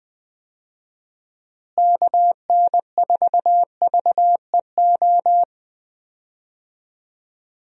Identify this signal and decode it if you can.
Morse code, heard as a keyed tone, about 20 wpm, 700 Hz. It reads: KN4VEO
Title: USA Call Signs - 6 Characters - Repeated 3x - 20wpm